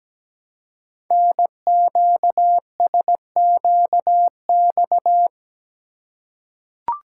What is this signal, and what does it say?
Morse code, heard as a keyed tone, about 17 wpm, 700 Hz. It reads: NQSQX
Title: Code Groups - 5 Letters - 17wpm